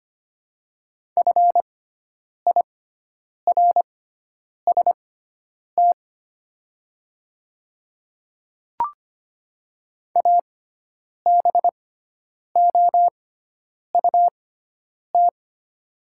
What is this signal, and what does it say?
Morse code, heard as a keyed tone, about 25 wpm, spaced as for 9 wpm, 700 Hz. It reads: FIRST ABOUT
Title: 6x Character Spacing - Top 300 Words - 25wpm